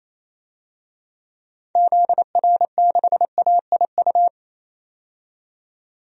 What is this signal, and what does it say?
Morse code, heard as a keyed tone, about 28 wpm, 700 Hz. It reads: ZR6AIU
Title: DXCC-100 Call Signs - 6 Characters - 28wpm